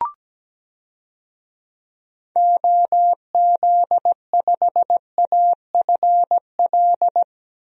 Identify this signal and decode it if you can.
Morse code, heard as a keyed tone, about 17 wpm, 700 Hz. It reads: OZ5AFL
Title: DXCC-100 Call Signs - 6 Characters - Repeated 3x - 17wpm